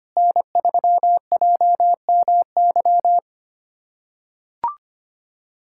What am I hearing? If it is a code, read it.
Morse code, heard as a keyed tone, about 25 wpm, 700 Hz. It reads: N3JMY
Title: USA Callsign DE Callsign 25wpm